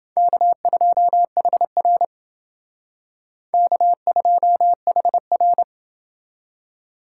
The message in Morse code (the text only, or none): K2HR K2HR